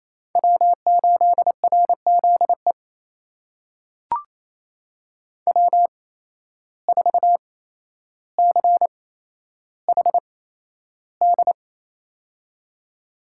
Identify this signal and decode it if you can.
Morse code, heard as a keyed tone, about 28 wpm, spaced as for 12 wpm, 700 Hz. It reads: W8RZE W4CHD
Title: USA Call Signs - 5 Characters - 8x Character Spacing - 28wpm